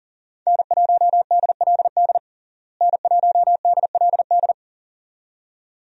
Morse code, heard as a keyed tone, about 40 wpm, 700 Hz. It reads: N1DLD N1DLD
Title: USA Call Signs - 5 Characters - Repeated 2x - 40wpm